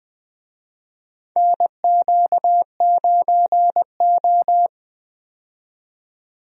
Morse code, heard as a keyed tone, about 20 wpm, 700 Hz. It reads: NQ9O
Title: USA Call Signs - 4 Characters - 20wpm